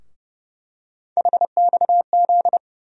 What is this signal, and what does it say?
Morse code, heard as a keyed tone, about 30 wpm, 700 Hz. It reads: HXZ
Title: USA Call Sign Postfixes - 3 Letters 30wpm